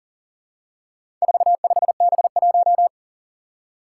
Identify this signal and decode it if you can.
Morse code, heard as a keyed tone, about 40 wpm, 700 Hz. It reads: 45B1
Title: Four Challenging Letter-Number Combinations 40wpm